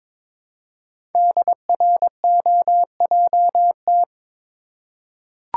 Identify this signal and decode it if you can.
Morse code, heard as a keyed tone, about 22 wpm, 700 Hz. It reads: DROJT
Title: Code Groups - 5 Letters - 22wpm